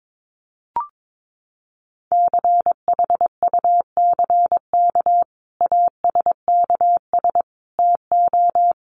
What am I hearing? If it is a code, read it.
Morse code, heard as a keyed tone, about 22 wpm, 700 Hz. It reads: CHUCK ASKS TO
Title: Additional Easy Phrases 22wpm